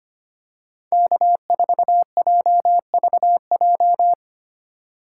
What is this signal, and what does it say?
Morse code, heard as a keyed tone, about 25 wpm, 700 Hz. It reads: K4JVJ
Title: USA Call Signs - 5 Characters - 3x Character Spacing - 25wpm